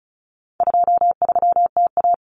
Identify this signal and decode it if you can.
Morse code, heard as a keyed tone, about 34 wpm, 700 Hz. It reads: 23TA
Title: Common Cut Numbers - 4-Digit 35wpm